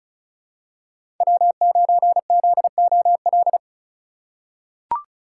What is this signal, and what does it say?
Morse code, heard as a keyed tone, about 35 wpm, 690 Hz. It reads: W9ZOL
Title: USA Call Signs - 5 Characters - 2x Character Spacing - 35wpm